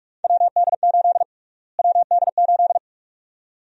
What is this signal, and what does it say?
Morse code, heard as a keyed tone, about 46 wpm, 700 Hz. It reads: WD8 WD8
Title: USA Call Sign Prefixes - Repeated 2x - 45wpm